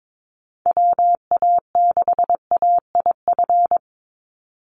Morse code, heard as a keyed tone, about 22 wpm, 700 Hz. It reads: WA6AIF
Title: USA Call Signs 22wpm